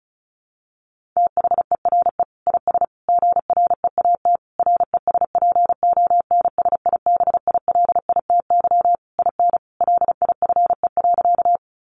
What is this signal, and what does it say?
Morse code, heard as a keyed tone, about 34 wpm, 700 Hz. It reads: THERE IS GREAT RESPONSIBILITY IN LIFE.
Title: Sentences from Top 1200 Words 35wpm